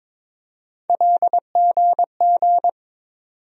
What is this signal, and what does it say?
Morse code, heard as a keyed tone, about 22 wpm, 700 Hz. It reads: LGG
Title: USA Call Sign Postfixes - 3 Letters 22wpm